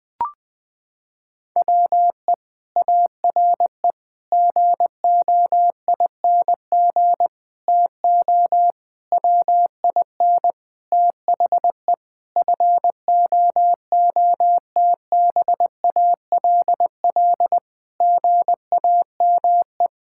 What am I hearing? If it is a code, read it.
Morse code, heard as a keyed tone, about 20 wpm, 700 Hz. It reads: WE ARE GOING TO WIN THE FOOTBALL GAME
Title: Sentences from Top 1600 Words 20wpm